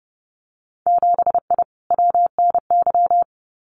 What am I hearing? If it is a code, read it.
Morse code, heard as a keyed tone, about 30 wpm, 700 Hz. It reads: 7I WNY
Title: Winter Field Day Exchanges Encore 30wpm